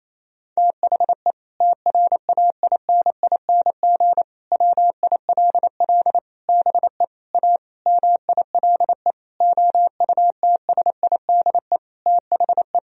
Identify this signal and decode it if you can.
Morse code, heard as a keyed tone, about 28 wpm, 700 Hz. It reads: THE TRAINING WILL BE A MILE OUTSIDE THE
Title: Sentences from Top 1000 Words - Review Part 4 - 28wpm